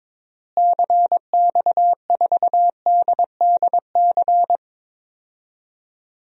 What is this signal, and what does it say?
Morse code, heard as a keyed tone, about 22 wpm, 700 Hz. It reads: CX4DDC
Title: DXCC-100 Call Signs - 6 Characters - Repeated 3x - 22wpm